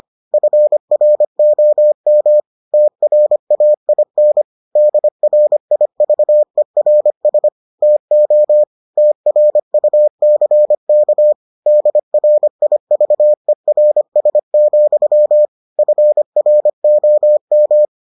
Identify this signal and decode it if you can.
Morse code, heard as a keyed tone, about 25 wpm, 600 Hz. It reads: FROM TRAIN DRIVERS TO TRUCK DRIVERS, FROM